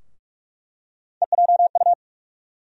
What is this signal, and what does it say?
Morse code, heard as a keyed tone, about 46 wpm, 700 Hz. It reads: EJU
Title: USA Call Sign Postfixes - 3 Letters 45wpm